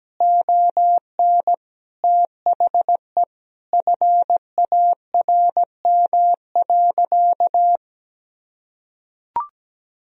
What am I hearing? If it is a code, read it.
Morse code, heard as a keyed tone, about 17 wpm, 700 Hz. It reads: ON THE FARM.
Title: Sentences from Top 1000 Words - Review Part 4 - 17wpm